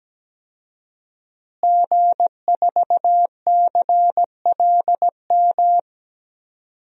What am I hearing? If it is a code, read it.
Morse code, heard as a keyed tone, about 17 wpm, 700 Hz. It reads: G4CLM
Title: DXCC-100 Call Signs - 5 Characters - Repeated 3x - 17wpm